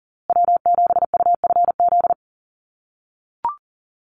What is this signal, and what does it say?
Morse code, heard as a keyed tone, about 40 wpm, 700 Hz. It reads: W7UFZ
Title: USA Call Signs Repeated 3x 40wpm